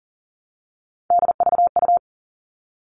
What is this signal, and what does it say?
Morse code, heard as a keyed tone, about 40 wpm, 700 Hz. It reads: DVU
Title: All Cut Numbers - 3-Digit 40wpm